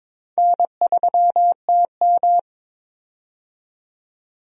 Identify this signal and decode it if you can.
Morse code, heard as a keyed tone, about 22 wpm, 700 Hz. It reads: N3TM
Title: USA Call Signs - 4 Characters - 22wpm